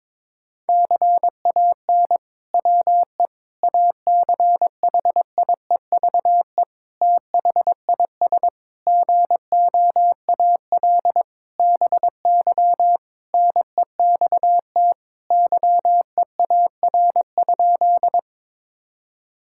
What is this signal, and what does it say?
Morse code, heard as a keyed tone, about 22 wpm, 700 Hz. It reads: CAN WE ACHIEVE THIS GOAL BY NEXT YEAR?
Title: Sentences from Top 1000 Words - Review Part 5 - 22wpm